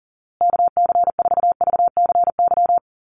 Morse code, heard as a keyed tone, about 37 wpm, 700 Hz. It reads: KC4VCY
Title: USA Call Signs Repeated 2x 40wpm